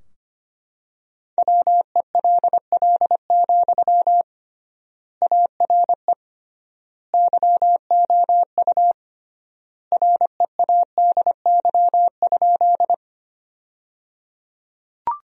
Morse code, heard as a keyed tone, about 25 wpm, 700 Hz. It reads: WELL, ARE YOU READY?
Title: 3x Word-Spacing - Sentences from Top 400 Words - 25wpm